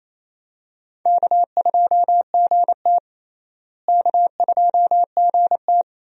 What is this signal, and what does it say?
Morse code, heard as a keyed tone, about 28 wpm, 700 Hz. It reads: K2GT K2GT
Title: USA Call Signs - 4 Characters - Repeated 3x - 28wpm